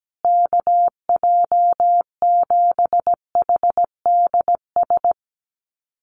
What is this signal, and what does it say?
Morse code, heard as a keyed tone, about 17 wpm, 700 Hz. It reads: KJ7HDS
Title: USA Call Signs Repeated 3x 17wpm